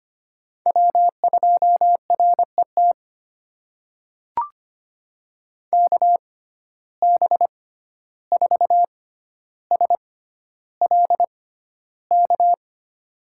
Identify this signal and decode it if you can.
Morse code, heard as a keyed tone, about 25 wpm, spaced as for 12 wpm, 700 Hz. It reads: W2RET KB4SLK
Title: USA Call Signs - 6x Character Spacing - 25wpm